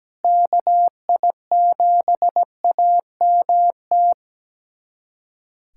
Morse code, heard as a keyed tone, about 17 wpm, 700 Hz. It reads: KI7AMT
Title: USA Callsign DE Callsign - 6 Characters - 17wpm